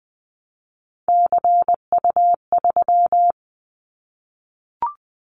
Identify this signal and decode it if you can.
Morse code, heard as a keyed tone, about 20 wpm, 700 Hz. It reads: CU3
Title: Callsign Prefixes for Top 50 DXCC - 20wpm